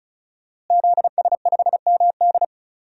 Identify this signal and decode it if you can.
Morse code, heard as a keyed tone, about 36 wpm, 700 Hz. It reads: ZS5MD
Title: DXCC-100 Call Signs - 5 Characters - Repeated 2x - 35wpm